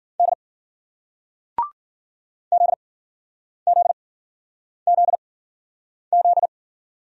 Morse code, heard as a keyed tone, about 56 wpm, spaced as for 9 wpm, 700 Hz. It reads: D ZZZZ